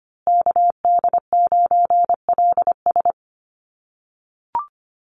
Morse code, heard as a keyed tone, about 25 wpm, 700 Hz. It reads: KD9LS
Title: USA Call Signs Repeated 2x 25wpm